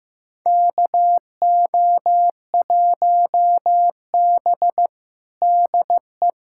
Morse code, heard as a keyed tone, about 16 wpm, 700 Hz. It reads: KO1B DE
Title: USA Callsign DE Callsign - 4 Characters - 15wpm